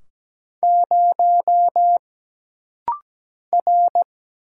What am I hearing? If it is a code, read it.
Morse code, heard as a keyed tone, about 17 wpm, 700 Hz. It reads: 0 R
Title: Single Letter-Number Alt - Rapid-Fire 17wpm